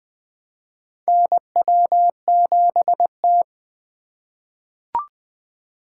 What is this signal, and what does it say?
Morse code, heard as a keyed tone, about 20 wpm, 700 Hz. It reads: NW7T
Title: USA Call Signs - 8x Character Spacing - 20wpm